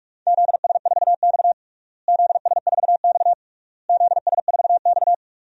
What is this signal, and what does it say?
Morse code, heard as a keyed tone, about 46 wpm, 700 Hz. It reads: ZS4X ZS4X ZS4X
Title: DXCC-100 Call Signs - 4 Characters - Repeated 3x - 45wpm